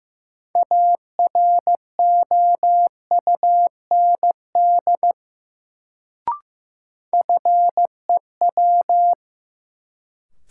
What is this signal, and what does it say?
Morse code, heard as a keyed tone, about 15 wpm, 700 Hz. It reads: AROUND FEW
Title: Sets of 2 Words - New Word Drill - 15wpm